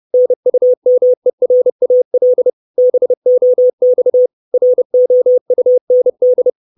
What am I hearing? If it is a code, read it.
Morse code, heard as a keyed tone, about 30 wpm, 500 Hz. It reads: NUMERAL BOX ROUND
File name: 632 Words 30WPM